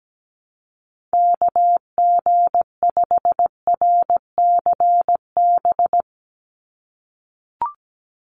Morse code, heard as a keyed tone, about 17 wpm, 700 Hz. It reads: KG5RCB
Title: USA Call Signs 17wpm